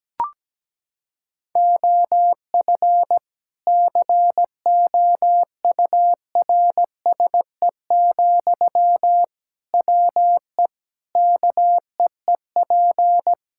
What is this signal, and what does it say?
Morse code, heard as a keyed tone, about 17 wpm, 700 Hz. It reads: OF COURSE, WE KEEP
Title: Sentences from Top 1600 Words 17wpm